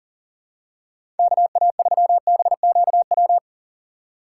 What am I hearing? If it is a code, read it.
Morse code, heard as a keyed tone, about 39 wpm, 700 Hz. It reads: KA3BQW
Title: USA Call Signs - 6 Characters - 2x Character Spacing - 40wpm